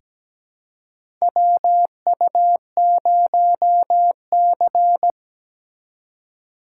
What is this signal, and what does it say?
Morse code, heard as a keyed tone, about 17 wpm, 700 Hz. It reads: WU0C
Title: USA Call Signs - 4 Characters - 2x Character Spacing - 17wpm